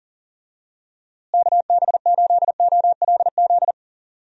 Morse code, heard as a keyed tone, about 39 wpm, 700 Hz. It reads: KB8OLZ
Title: USA Call Signs - 6 Characters - Repeated 3x - 40wpm